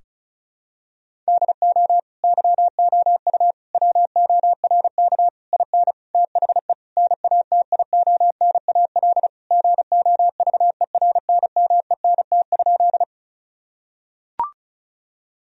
Morse code, heard as a keyed tone, about 36 wpm, 700 Hz. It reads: DO YOU WORK IN THE NATIONAL GOVERNMENT?
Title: Sentences from Top 1000 Words - Review Part 4 - 35wpm